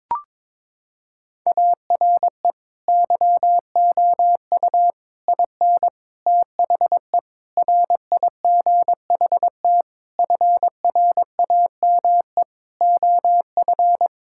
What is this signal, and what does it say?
Morse code, heard as a keyed tone, about 22 wpm, 700 Hz. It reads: ARE YOU IN THE RIGHT FRAME OF
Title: Sentences from Top 1000 Words - Review Part 6 - 22wpm